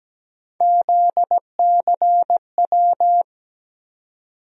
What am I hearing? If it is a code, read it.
Morse code, heard as a keyed tone, about 17 wpm, 700 Hz. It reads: ZCW
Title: USA Call Sign Postfixes - 3 Letters 17wpm